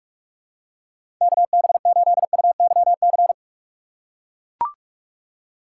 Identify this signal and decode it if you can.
Morse code, heard as a keyed tone, about 44 wpm, 700 Hz. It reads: KB8UYC
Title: USA Call Signs - 2x Character Spacing - 45wpm